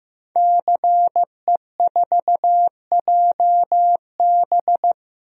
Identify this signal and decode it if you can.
Morse code, heard as a keyed tone, about 15 wpm, 700 Hz. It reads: CE4JB
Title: DXCC-100 Call Signs - 5 Characters - 15wpm